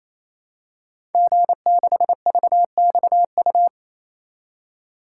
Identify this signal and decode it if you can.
Morse code, heard as a keyed tone, about 27 wpm, 700 Hz. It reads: G6VXU
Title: Code Groups - 5 Letter:Numbers - 28wpm